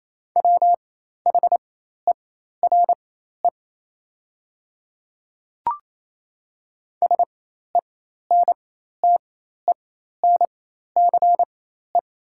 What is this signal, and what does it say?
Morse code, heard as a keyed tone, about 28 wpm, spaced as for 13 wpm, 690 Hz. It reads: WHERE SENTENCE
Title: 4x Character Spacing - Top 200 Words - 28wpm